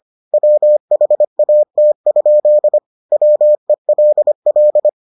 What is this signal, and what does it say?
Morse code, heard as a keyed tone, about 25 wpm, 600 Hz. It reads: WHAT? WELL